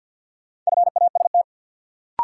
Morse code, heard as a keyed tone, about 53 wpm, 700 Hz. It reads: FAST
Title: Sentences from Top 400 Words - New Word Drill - 50wpm